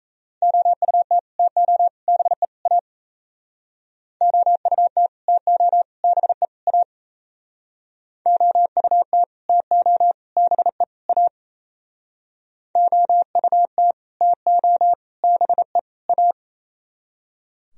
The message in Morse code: OUT TO BE A OUT TO BE A OUT TO BE A OUT TO BE A

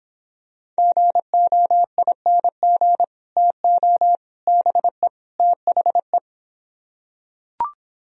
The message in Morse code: GOING TO BE THE

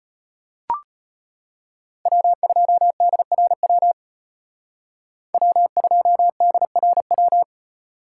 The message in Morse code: W2DRW W2DRW

